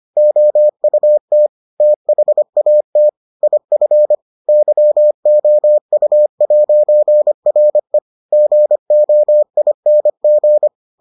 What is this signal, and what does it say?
Morse code, heard as a keyed tone, about 25 wpm, 600 Hz. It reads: OUT THAT IF YOU'RE GOING